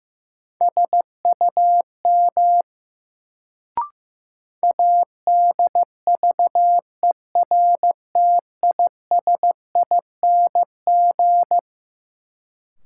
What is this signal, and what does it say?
Morse code, heard as a keyed tone, about 15 wpm, 700 Hz. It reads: SUM ADVERTISING
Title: Sentences from Top 1000 Words - Review Part 6 - New Word Drill - 15wpm